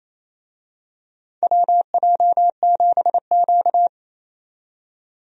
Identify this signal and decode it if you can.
Morse code, heard as a keyed tone, about 28 wpm, 700 Hz. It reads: WJ7Q
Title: USA Call Signs - 4 Characters - 28wpm